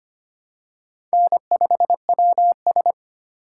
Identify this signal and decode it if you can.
Morse code, heard as a keyed tone, about 25 wpm, 700 Hz. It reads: N5WS
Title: USA Call Signs - 4 Characters - 2x Character Spacing - 25wpm